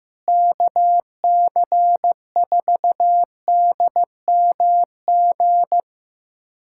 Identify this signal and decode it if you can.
Morse code, heard as a keyed tone, about 15 wpm, 700 Hz. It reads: KC4DMG
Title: USA Callsign DE Callsign - 6 Characters - 15wpm